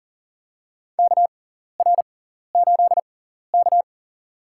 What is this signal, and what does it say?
Morse code, heard as a keyed tone, about 40 wpm, spaced as for 14 wpm, 700 Hz. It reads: KR8K
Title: USA Call Signs - 4 Characters - 6x Character Spacing - 40wpm